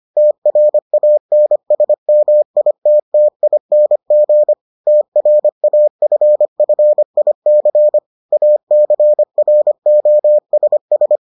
Morse code, heard as a keyed tone, about 25 wpm, 600 Hz. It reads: TRANSMITTING TRAFFIC ACROSS